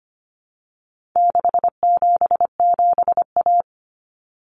Morse code, heard as a keyed tone, about 25 wpm, 700 Hz. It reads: 677A
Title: Common Cut Numbers - 4-Digit 25wpm